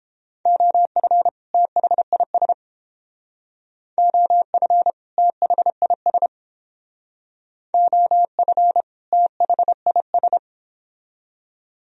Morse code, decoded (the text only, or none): OF THIS OF THIS OF THIS